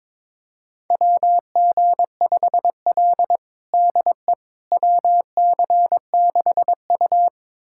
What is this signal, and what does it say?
Morse code, heard as a keyed tone, about 22 wpm, 700 Hz. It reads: WG5L DE WC6U
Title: USA Callsign DE Callsign - 4 Characters - 22wpm